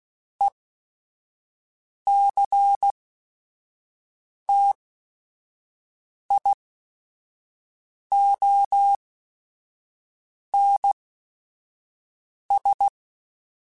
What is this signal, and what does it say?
Morse code, heard as a keyed tone, about 16 wpm, spaced as for 5 wpm, 800 Hz. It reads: ECTIONS
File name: Technical Subjects / Technical Subjects-mp3s / Sw Pwr Sup @5